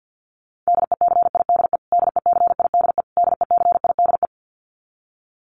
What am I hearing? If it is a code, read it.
Morse code, heard as a keyed tone, about 50 wpm, 700 Hz. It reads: DECIDE DECIDE DECIDE
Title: Top 500 Words Repeated 50wpm